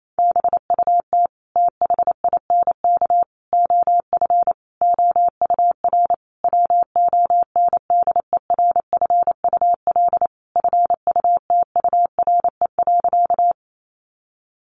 Morse code, heard as a keyed tone, about 28 wpm, 700 Hz. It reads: BUT THINK OF OUR WONDERFUL FUTURE.
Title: Sentences from Top 1100 Words 28wpm